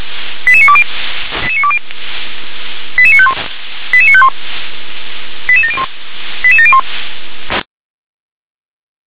Five-tone selective call (ZVEI), heard as quick runs of five tones, tones of about 70 ms. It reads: 88920 88951 88951 88701 88701